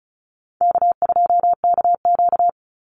Morse code, heard as a keyed tone, about 34 wpm, 700 Hz. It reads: K2KQ
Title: Call Signs - 35wpm